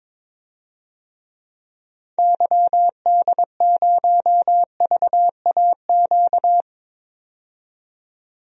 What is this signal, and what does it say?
Morse code, heard as a keyed tone, about 22 wpm, 700 Hz. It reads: YD0VAQ
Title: DXCC-100 Call Signs - 6 Characters - 22wpm